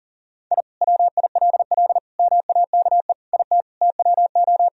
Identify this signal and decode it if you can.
Morse code, heard as a keyed tone, about 40 wpm, 700 Hz. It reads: I WILL MAKE IT TWO
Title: Sentences from Top 1000 Words - Review Part 2 - 40wpm